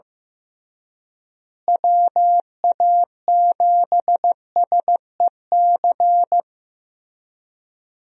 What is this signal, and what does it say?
Morse code, heard as a keyed tone, about 15 wpm, 700 Hz. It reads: WA7SEC
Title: USA Call Signs - 6 Characters - Repeated 3x - 15wpm